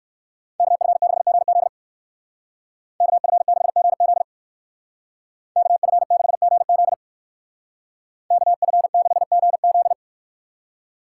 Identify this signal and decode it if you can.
Morse code, heard as a keyed tone, about 59 wpm, 700 Hz. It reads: KF6GZ KF6GZ KF6GZ KF6GZ